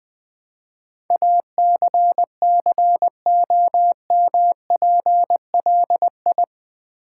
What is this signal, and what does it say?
Morse code, heard as a keyed tone, about 20 wpm, 700 Hz. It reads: ACCOMPLI
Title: Build A Word - 4 Letters At A Time - 10-Letter Words - 20wpm